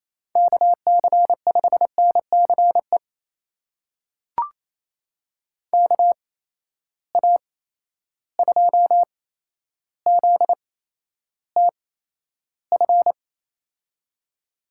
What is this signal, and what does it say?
Morse code, heard as a keyed tone, about 28 wpm, spaced as for 12 wpm, 700 Hz. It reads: KC5NCE KA2ZTF
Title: USA Call Signs - 6 Characters - 8x Character Spacing - 28wpm